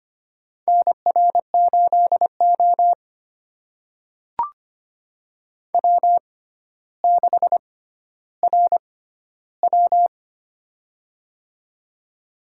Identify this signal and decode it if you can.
Morse code, heard as a keyed tone, about 25 wpm, spaced as for 13 wpm, 700 Hz. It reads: NR8O W6RW
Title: USA Call Signs - 6x Character Spacing - 25wpm